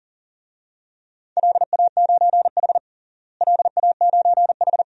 Morse code, heard as a keyed tone, about 37 wpm, 700 Hz. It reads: LA9H LA9H
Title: DXCC-100 Call Signs - 4 Characters - Repeated 3x - 40wpm